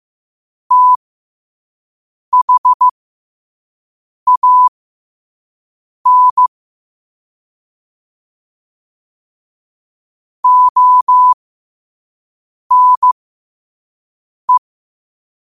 Morse code, heard as a keyed tone, about 15 wpm, spaced as for 5 wpm, 1000 Hz. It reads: THAN ONE
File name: qotdfiles / 5wpm